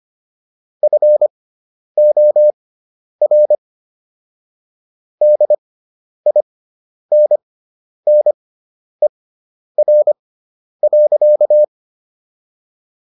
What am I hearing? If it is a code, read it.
Morse code, heard as a keyed tone, about 25 wpm, spaced as for 10 wpm, 600 Hz. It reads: FOR DINNER.